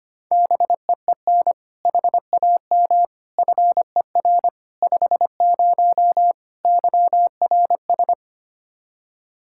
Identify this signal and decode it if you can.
Morse code, heard as a keyed tone, about 25 wpm, 700 Hz. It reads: BEEN HAM FER 50 YRS
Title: Practice QSO - Exchange 3 - More Info - 25wpm